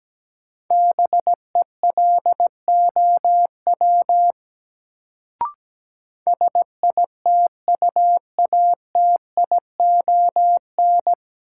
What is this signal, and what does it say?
Morse code, heard as a keyed tone, about 17 wpm, 700 Hz. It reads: BELOW SITUATION
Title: Sentences from Top 1200 Words - New Word Drill Alt - 17wpm